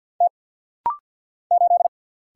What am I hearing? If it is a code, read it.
Morse code, heard as a keyed tone, about 50 wpm, 700 Hz. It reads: T 8
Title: Single Letter-Number - Rapid-Fire 50wpm